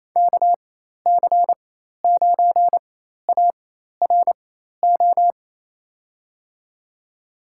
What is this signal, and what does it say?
Morse code, heard as a keyed tone, about 28 wpm, spaced as for 13 wpm, 700 Hz. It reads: KC9ARO
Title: USA Call Signs - 4x Character Spacing - 28wpm